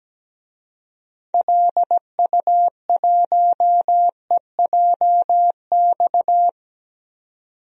Morse code, heard as a keyed tone, about 17 wpm, 700 Hz. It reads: LU1EJX
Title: DXCC-100 Call Signs - 6 Characters - 17wpm